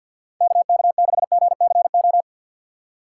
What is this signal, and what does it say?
Morse code, heard as a keyed tone, about 50 wpm, 700 Hz. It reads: KK6GCO
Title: USA Call Signs - 4x Character Spacing - 50wpm